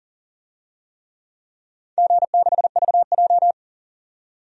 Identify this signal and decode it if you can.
Morse code, heard as a keyed tone, about 40 wpm, 700 Hz. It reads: G6VJ
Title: DXCC-100 Call Signs - 4 Characters - 40wpm